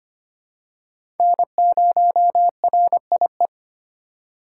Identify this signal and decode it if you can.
Morse code, heard as a keyed tone, about 25 wpm, 700 Hz. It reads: N0RIE
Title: USA Call Signs - 5 Characters - 3x Character Spacing - 25wpm